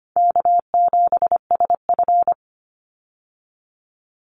Morse code, heard as a keyed tone, about 25 wpm, 700 Hz. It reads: K7SF
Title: Easy Call Signs Repeated 2x - 25wpm